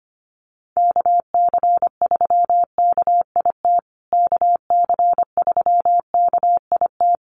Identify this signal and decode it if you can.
Morse code, heard as a keyed tone, about 25 wpm, 700 Hz. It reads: KC3KIT KC3KIT
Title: USA Call Signs Repeated 3x 25wpm